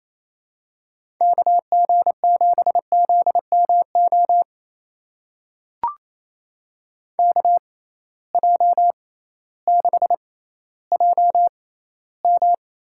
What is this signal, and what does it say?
Morse code, heard as a keyed tone, about 28 wpm, spaced as for 15 wpm, 700 Hz. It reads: KG7ZMO KJ6JM
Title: USA Call Signs - 6 Characters - 6x Character Spacing - 28wpm